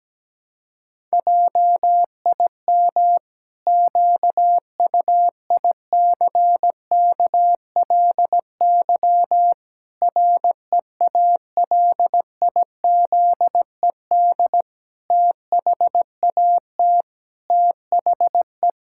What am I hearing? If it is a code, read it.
Morse code, heard as a keyed tone, about 17 wpm, 700 Hz. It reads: JIM QUICKLY REALIZED THAT THE
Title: Pangram Challenge v2 - 17wpm